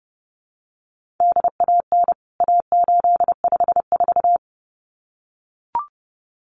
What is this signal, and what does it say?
Morse code, heard as a keyed tone, about 30 wpm, 700 Hz. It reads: DAN A854
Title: CWT Exchange 30wpm